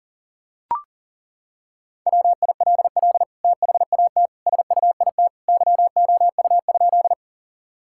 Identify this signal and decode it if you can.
Morse code, heard as a keyed tone, about 40 wpm, 700 Hz. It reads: WILL THAT SUIT YOU?
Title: Sentences from Top 1600 Words 40wpm